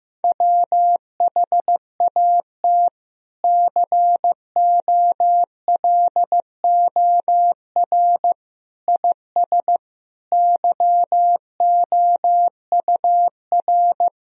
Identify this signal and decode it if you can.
Morse code, heard as a keyed tone, about 15 wpm, 700 Hz. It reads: WHAT COLOR IS YOUR
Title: Sentences from Top 1000 Words - Review Part 3 - 15wpm